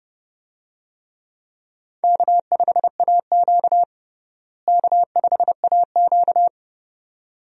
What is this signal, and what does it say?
Morse code, heard as a keyed tone, about 30 wpm, 700 Hz. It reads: K5AQ K5AQ
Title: USA Call Signs - 4 Characters - Repeated 2x - 30wpm